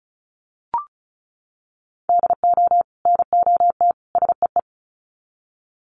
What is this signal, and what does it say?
Morse code, heard as a keyed tone, about 34 wpm, 700 Hz. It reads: DO NOT SEE
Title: Sets of 3 Words 35wpm